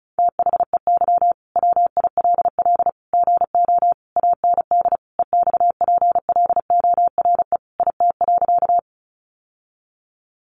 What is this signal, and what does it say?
Morse code, heard as a keyed tone, about 34 wpm, 700 Hz. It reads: THEY WILL GO AND EXPLORE IT.
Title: Sentences from Top 1100 Words 35wpm